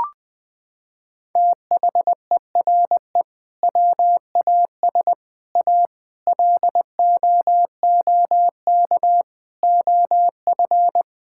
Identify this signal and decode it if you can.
Morse code, heard as a keyed tone, about 20 wpm, 700 Hz. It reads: THERE WAS A LOOK OF
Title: Sentences from Top 1600 Words 20wpm